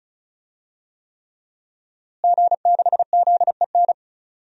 Morse code, heard as a keyed tone, about 36 wpm, 700 Hz. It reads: G6ZEN
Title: DXCC-100 Call Signs - 5 Characters - Repeated 3x - 35wpm